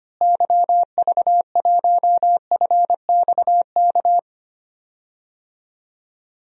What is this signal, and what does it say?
Morse code, heard as a keyed tone, about 25 wpm, 700 Hz. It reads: YV1FXK
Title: DXCC-100 Call Signs - 6 Characters - Repeated 3x - 25wpm